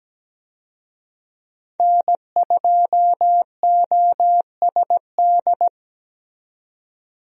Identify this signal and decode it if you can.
Morse code, heard as a keyed tone, about 17 wpm, 700 Hz. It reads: N2OSD
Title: USA Call Signs - 5 Characters - 17wpm